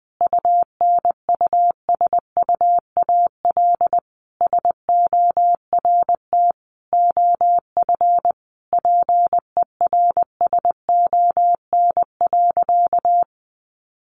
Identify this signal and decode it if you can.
Morse code, heard as a keyed tone, about 20 wpm, 700 Hz. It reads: UNUSUAL SORT OF PERSON.
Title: Sentences from Top 1400 Words 20wpm